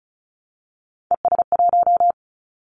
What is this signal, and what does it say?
Morse code, heard as a keyed tone, about 35 wpm, 700 Hz. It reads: ES1
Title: Callsign Prefixes for Top 100 DXCC - 35wpm